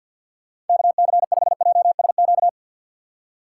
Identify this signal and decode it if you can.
Morse code, heard as a keyed tone, about 51 wpm, 700 Hz. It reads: KC5JSQ
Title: USA Call Signs - 3x Character Spacing - 50wpm